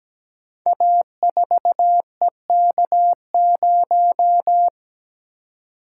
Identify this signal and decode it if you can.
Morse code, heard as a keyed tone, about 17 wpm, 700 Hz. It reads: A4EK0
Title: Code Groups - 5 Letter:Numbers - 17wpm